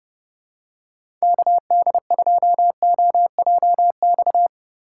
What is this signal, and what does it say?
Morse code, heard as a keyed tone, about 32 wpm, 700 Hz. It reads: KD2OJX
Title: USA Call Signs - 4x Character Spacing - 30wpm